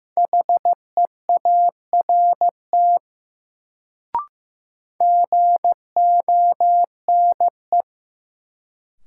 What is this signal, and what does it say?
Morse code, heard as a keyed tone, about 15 wpm, 700 Hz. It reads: HEART GONE
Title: Sentences from Top 1200 Words - New Word Drill Alt - 15wpm